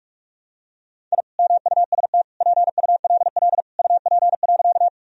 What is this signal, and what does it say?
Morse code, heard as a keyed tone, about 46 wpm, 700 Hz. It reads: I MUST PULL UP.
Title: Sentences from Top 1000 Words - Review Part 2 - 45wpm